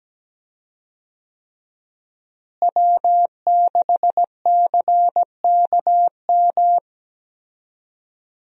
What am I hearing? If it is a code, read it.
Morse code, heard as a keyed tone, about 17 wpm, 700 Hz. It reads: W6CKM
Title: USA Call Signs - 5 Characters - Repeated 3x - 17wpm